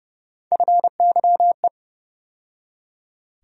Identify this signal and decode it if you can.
Morse code, heard as a keyed tone, about 32 wpm, 700 Hz. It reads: FYE
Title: USA Call Sign Postfixes - 3 Letters 30wpm